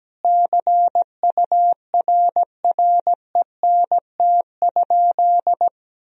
Morse code, heard as a keyed tone, about 17 wpm, 700 Hz. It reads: CURRENT?
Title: Sentences from Top 1000 Words - Review Part 4 - 17wpm